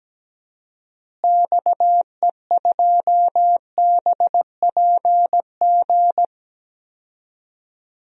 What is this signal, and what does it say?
Morse code, heard as a keyed tone, about 17 wpm, 700 Hz. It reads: XE2BPG
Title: DXCC-100 Call Signs - 6 Characters - Repeated 3x - 17wpm